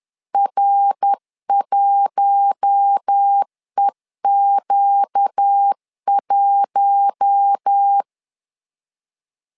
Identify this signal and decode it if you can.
Morse code, heard as a keyed tone, about 10 wpm, 800 Hz. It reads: R1EQ1
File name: LEZIONI / 4 Q1 10wpm 2448